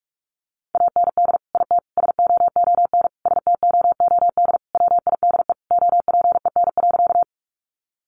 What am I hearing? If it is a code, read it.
Morse code, heard as a keyed tone, about 44 wpm, 700 Hz. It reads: AND IT SOON STOOD WIDE OPEN.
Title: Sentences from Top 900 Words 45wpm